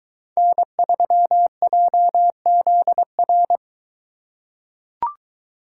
Morse code, heard as a keyed tone, about 23 wpm, 700 Hz. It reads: N3JZR